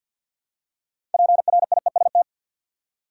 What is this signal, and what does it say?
Morse code, heard as a keyed tone, about 53 wpm, 700 Hz. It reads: PRIEST
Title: Six Letter Words - 50wpm